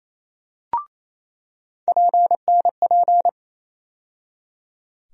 Morse code, heard as a keyed tone, about 28 wpm, 700 Hz. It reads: PNP
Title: USA Call Sign Postfixes - 3 Letters 28wpm